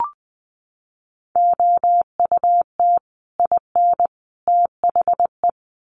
Morse code, heard as a keyed tone, about 20 wpm, 700 Hz. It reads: OUT IN THE
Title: Sets of 3 Words 20wpm